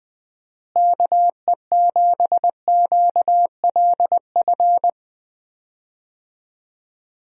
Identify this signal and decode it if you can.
Morse code, heard as a keyed tone, about 20 wpm, 700 Hz. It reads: KE7QLF
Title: USA Call Signs - 6 Characters - Repeated 2x - 20wpm